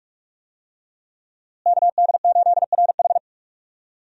Morse code, heard as a keyed tone, about 46 wpm, 700 Hz. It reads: KD8RH
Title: USA Call Signs - 5 Characters - Repeated 2x - 45wpm